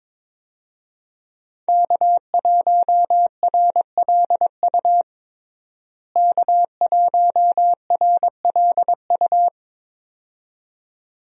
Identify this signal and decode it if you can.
Morse code, heard as a keyed tone, about 21 wpm, 700 Hz. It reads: K1RLU K1RLU
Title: USA Call Signs - 5 Characters - Repeated 2x - 22wpm